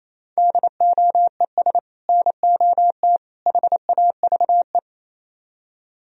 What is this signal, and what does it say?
Morse code, heard as a keyed tone, about 37 wpm, 700 Hz. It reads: DOES NOT HAVE